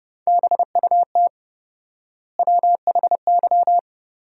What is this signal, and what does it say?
Morse code, heard as a keyed tone, about 30 wpm, 700 Hz. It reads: BUT WHY
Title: 4x Word-Spacing - Sentences from Top 400 Words - 30wpm